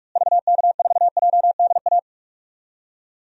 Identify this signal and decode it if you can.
Morse code, heard as a keyed tone, about 46 wpm, 700 Hz. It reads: VK4JDA
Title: DXCC-100 Call Signs - 6 Characters - Repeated 3x - 45wpm